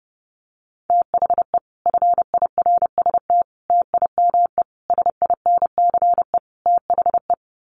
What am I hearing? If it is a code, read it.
Morse code, heard as a keyed tone, about 30 wpm, 700 Hz. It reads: THE FIRST TIME SINCE THE
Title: Sets of 5 Words Encore 30wpm